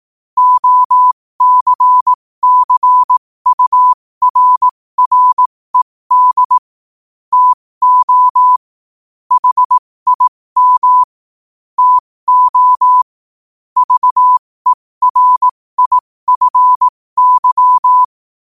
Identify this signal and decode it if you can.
Morse code, heard as a keyed tone, about 18 wpm, spaced as for 15 wpm, 1000 Hz. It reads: OCCURRED TO HIM TO VERIFY
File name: qotdfiles / 15wpm